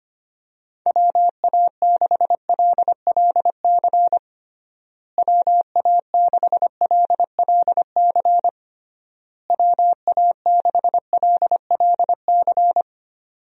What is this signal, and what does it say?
Morse code, heard as a keyed tone, about 25 wpm, 700 Hz. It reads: WA6LLC WA6LLC WA6LLC
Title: USA Call Signs - 6 Characters - Repeated 3x - 25wpm